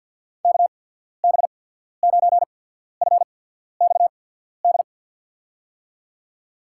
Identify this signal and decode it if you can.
Morse code, heard as a keyed tone, about 52 wpm, 700 Hz. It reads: K B 9 F X D
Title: USA Call Signs - 6 Characters - 8x Character Spacing - 50wpm